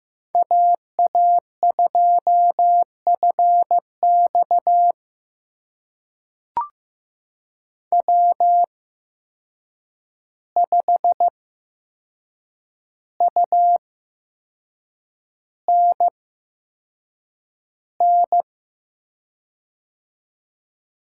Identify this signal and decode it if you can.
Morse code, heard as a keyed tone, about 15 wpm, spaced as for 6 wpm, 700 Hz. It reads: AA2FX W5UNN
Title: USA Call Signs - 5 Characters - 8x Character Spacing - 15wpm